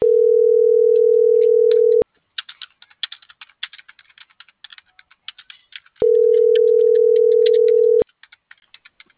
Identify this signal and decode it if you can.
Ringback tone (North America), 440 + 480 Hz.